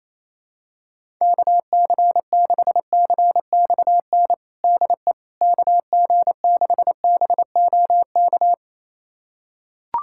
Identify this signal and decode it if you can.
Morse code, heard as a keyed tone, about 28 wpm, 700 Hz. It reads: KC6CXN DE KG6BOK
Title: USA Callsign DE Callsign - 6 Characters - 28wpm